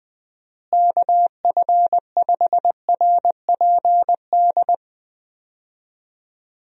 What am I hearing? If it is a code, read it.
Morse code, heard as a keyed tone, about 19 wpm, 700 Hz. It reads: KF5RPD
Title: USA Call Signs - 6 Characters - Repeated 3x - 20wpm